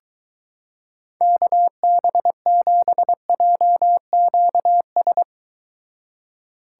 Morse code, heard as a keyed tone, about 23 wpm, 700 Hz. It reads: KB7JQS